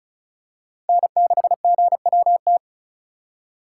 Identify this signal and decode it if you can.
Morse code, heard as a keyed tone, about 35 wpm, 700 Hz. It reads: N6GWT
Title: USA Call Signs - 5 Characters - 2x Character Spacing - 35wpm